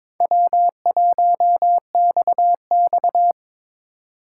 Morse code, heard as a keyed tone, about 22 wpm, 700 Hz. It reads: W1XX
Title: Speed Racing - USA Call Signs 22wpm